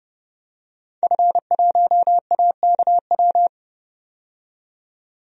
Morse code, heard as a keyed tone, about 30 wpm, 700 Hz. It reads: F1AKW
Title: DXCC-100 Call Signs - 5 Characters - 30wpm